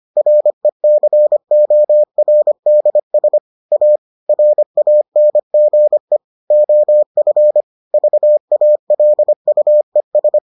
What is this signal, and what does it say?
Morse code, heard as a keyed tone, about 25 wpm, 600 Hz. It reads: RECORDS A RANGE OF VALUES